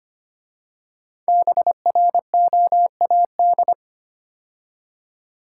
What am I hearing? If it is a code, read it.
Morse code, heard as a keyed tone, about 25 wpm, 700 Hz. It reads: BROAD
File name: 5 Letter Words - First 3 Letters Given - 25wpm